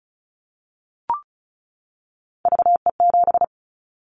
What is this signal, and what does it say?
Morse code, heard as a keyed tone, about 36 wpm, 700 Hz. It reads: VE7
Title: Callsign Prefixes for Top 25 DXCC - 35wpm